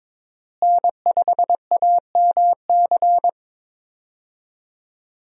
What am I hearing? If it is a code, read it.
Morse code, heard as a keyed tone, about 22 wpm, 700 Hz. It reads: N5AMC
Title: USA Call Signs - 5 Characters - Repeated 2x - 22wpm